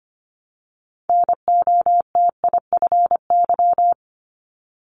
Morse code, heard as a keyed tone, about 25 wpm, 700 Hz. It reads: NOTIFY
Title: Multi-letter Phonemes Repeated 3x 25wpm